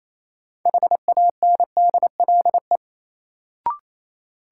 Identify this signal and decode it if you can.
Morse code, heard as a keyed tone, about 28 wpm, 700 Hz. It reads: HANDLE
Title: Sentences from Top 1100 Words - New Word Drill Alt - 28wpm